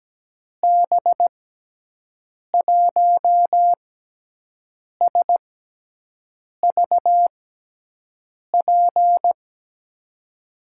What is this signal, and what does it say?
Morse code, heard as a keyed tone, about 17 wpm, spaced as for 6 wpm, 700 Hz. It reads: B1SVP
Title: USA Call Signs - 6x Character Spacing - 17wpm